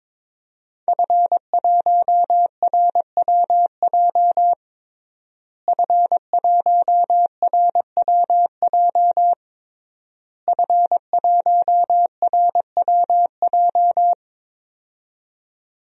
Morse code, heard as a keyed tone, about 22 wpm, 700 Hz. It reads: F1RWJ F1RWJ F1RWJ
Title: DXCC-100 Call Signs - 5 Characters - Repeated 3x - 22wpm